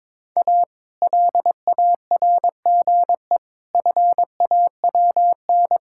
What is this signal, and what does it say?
Morse code, heard as a keyed tone, about 22 wpm, 700 Hz. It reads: A LARGE FAWN
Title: Pangram Challenge v2 - 22wpm